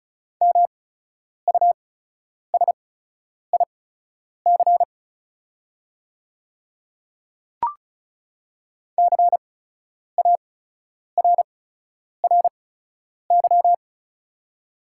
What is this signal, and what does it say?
Morse code, heard as a keyed tone, about 36 wpm, spaced as for 10 wpm, 700 Hz. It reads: MUSIC CARRY
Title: 8x Character Spacing - Top 500 Words - 35wpm